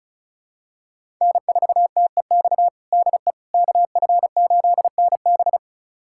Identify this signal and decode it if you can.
Morse code, heard as a keyed tone, about 36 wpm, 700 Hz. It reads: N4TEX DE KF8NB
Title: USA Callsign DE Callsign - 5 Characters - 35wpm